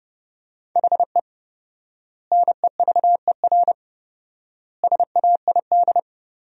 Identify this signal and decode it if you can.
Morse code, heard as a keyed tone, about 32 wpm, 700 Hz. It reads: HE NEVER SAID
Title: 4x Word-Spacing - Sentences from Top 500 Words - 30wpm